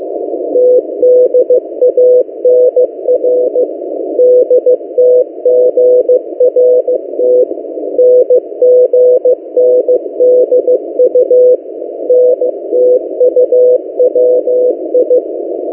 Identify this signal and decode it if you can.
Morse code, heard as a keyed tone, about 15 wpm, 500 Hz. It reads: TDANR DTGRT NGNDU NTUWI
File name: M08a- -1355utc-8096khz